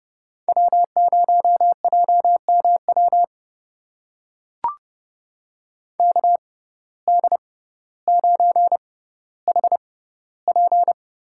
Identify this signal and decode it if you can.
Morse code, heard as a keyed tone, about 30 wpm, spaced as for 15 wpm, 700 Hz. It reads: W0JMW KD9HP
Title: USA Call Signs - 6x Character Spacing - 30wpm